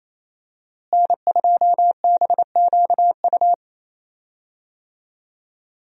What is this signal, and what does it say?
Morse code, heard as a keyed tone, about 28 wpm, 700 Hz. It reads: N2BQU
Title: USA Call Signs - 5 Characters - Repeated 3x - 28wpm